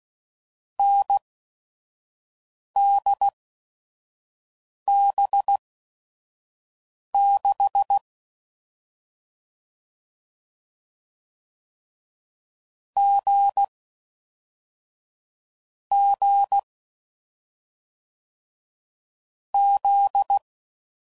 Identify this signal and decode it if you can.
Morse code, heard as a keyed tone, about 16 wpm, spaced as for 5 wpm, 800 Hz. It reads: NDB6 G G Z
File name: Combinations / Difficult Combos / awj1 ndb6 gz7